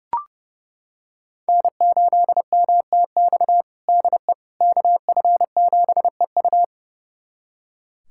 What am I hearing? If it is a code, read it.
Morse code, heard as a keyed tone, about 30 wpm, 700 Hz. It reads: N8MTX DE KF7EU
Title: USA Callsign DE Callsign - 5 Characters - 30wpm